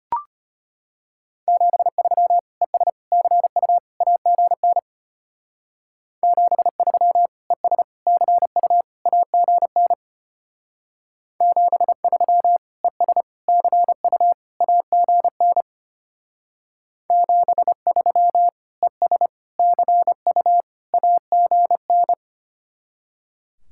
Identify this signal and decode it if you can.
Morse code, heard as a keyed tone, about 30 wpm, 700 Hz. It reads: 73 ES CU AGN 73 ES CU AGN 73 ES CU AGN 73 ES CU AGN
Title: Speed-Racing Common QSO Phrases - 25wpm